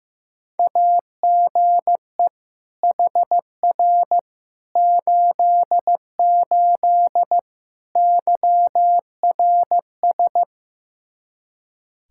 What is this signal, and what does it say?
Morse code, heard as a keyed tone, about 15 wpm, 700 Hz. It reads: AGE HR 88 YRS
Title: Practice QSO - Exchange 3 - More Info - 15wpm